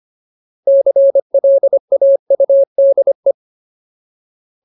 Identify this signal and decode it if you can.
Morse code, heard as a keyed tone, about 25 wpm, 550 Hz. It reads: CLAUDE
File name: Ultimate Word List / 25WPM Ultimate